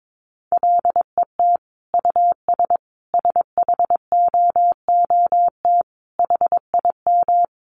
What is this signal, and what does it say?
Morse code, heard as a keyed tone, about 22 wpm, 700 Hz. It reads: LET US SHOOT HIM
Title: Sentences from Top 1200 Words 22wpm